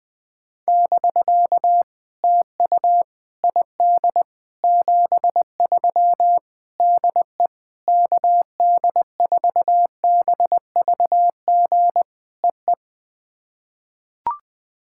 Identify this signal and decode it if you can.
Morse code, heard as a keyed tone, about 20 wpm, 700 Hz. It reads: <BK> TU ID 73 DE KD4BVG EE
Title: POTA QSO Practice - 20wpm